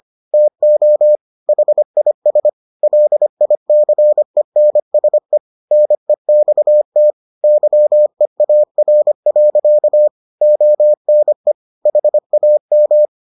Morse code, heard as a keyed tone, about 25 wpm, 600 Hz. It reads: TO HIS LICENSE NEXT YEAR. ONE HAM